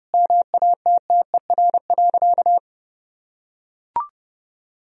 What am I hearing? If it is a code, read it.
Morse code, heard as a keyed tone, about 32 wpm, 700 Hz. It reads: MATTER.
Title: Sentences from Top 1000 Words - Review Part 3 - 30wpm